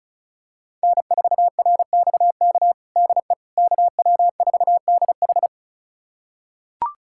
Morse code, heard as a keyed tone, about 36 wpm, 700 Hz. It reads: N4RXK DE KW4DH
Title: USA Callsign DE Callsign - 5 Characters - 35wpm